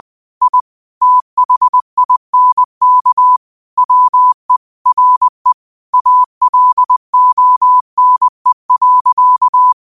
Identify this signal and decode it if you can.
Morse code, heard as a keyed tone, about 20 wpm, 1000 Hz. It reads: I THINK WE RE ALONE.
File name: qotdfiles / 20wpm